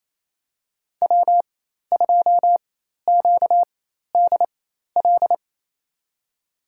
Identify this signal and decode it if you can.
Morse code, heard as a keyed tone, about 28 wpm, spaced as for 13 wpm, 700 Hz. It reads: W2QDL
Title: USA Call Signs - 4x Character Spacing - 28wpm